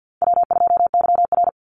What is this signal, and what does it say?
Morse code, heard as a keyed tone, about 50 wpm, 700 Hz. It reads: W2YR
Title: Easy Call Signs - 50wpm